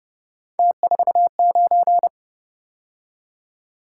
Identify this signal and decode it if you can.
Morse code, heard as a keyed tone, about 32 wpm, 700 Hz. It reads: T49